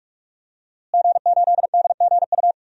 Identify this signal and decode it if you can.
Morse code, heard as a keyed tone, about 46 wpm, 700 Hz. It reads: G8DGU
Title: DXCC-100 Call Signs - 5 Characters - Repeated 3x - 45wpm